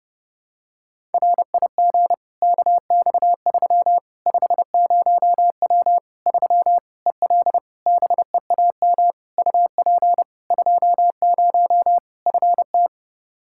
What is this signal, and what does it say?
Morse code, heard as a keyed tone, about 30 wpm, 700 Hz. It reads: RIG KX3 50W 3 EL BEAM UP 20 FT
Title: Practice QSO - Exchange 2 - Info - 30wpm